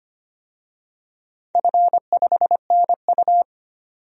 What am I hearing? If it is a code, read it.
Morse code, heard as a keyed tone, about 25 wpm, 700 Hz. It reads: F5NU
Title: DXCC-100 Call Signs - 4 Characters - 25wpm